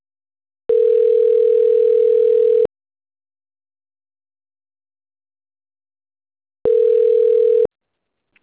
Ringback tone (North America), 440 + 480 Hz.
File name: autotestagent / 10-24-34.136..callcentertransfer. .7b1873f1-4b7f-4e26-9c71